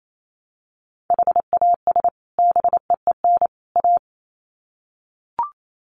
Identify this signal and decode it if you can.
Morse code, heard as a keyed tone, about 28 wpm, 700 Hz. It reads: HAS BEEN A
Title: Sets of 3 Words 28wpm